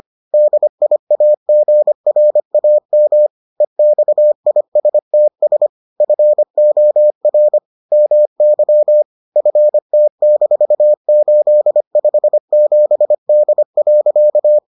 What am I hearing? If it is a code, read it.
Morse code, heard as a keyed tone, about 25 wpm, 600 Hz. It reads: DIAGRAM EXISTS FOR MY FT-857D.